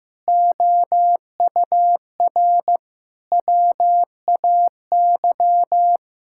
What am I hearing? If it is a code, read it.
Morse code, heard as a keyed tone, about 15 wpm, 700 Hz. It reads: OUR WAY
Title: Sentences from Top 1600 Words 15wpm